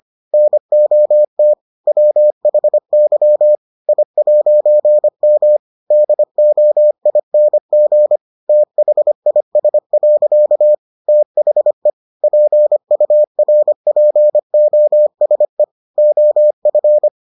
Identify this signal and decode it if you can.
Morse code, heard as a keyed tone, about 25 wpm, 600 Hz. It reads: NOT WHY I'M DOING THIS. THE PURPOSE OF